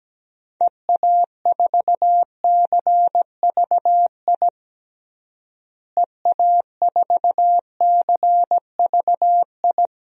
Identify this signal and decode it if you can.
Morse code, heard as a keyed tone, about 17 wpm, 700 Hz. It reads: EA4CVI EA4CVI
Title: DXCC-100 Call Signs - 6 Characters - Repeated 2x - 17wpm